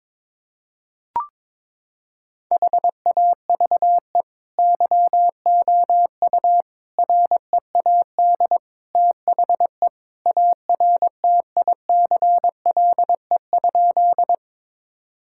Morse code, heard as a keyed tone, about 21 wpm, 700 Hz. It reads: HAVE YOU READ THE ARTICLE?
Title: Sentences from Top 1000 Words - Review Part 5 - 22wpm